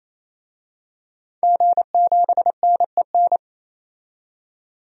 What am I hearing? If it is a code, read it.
Morse code, heard as a keyed tone, about 28 wpm, 700 Hz. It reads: G7NEN
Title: DXCC-100 Call Signs - 5 Characters - 28wpm